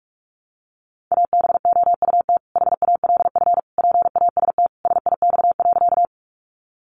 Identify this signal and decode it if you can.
Morse code, heard as a keyed tone, about 44 wpm, 700 Hz. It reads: ABOUT HALF PAST SIX.
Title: Sentences from Top 600 Words 45wpm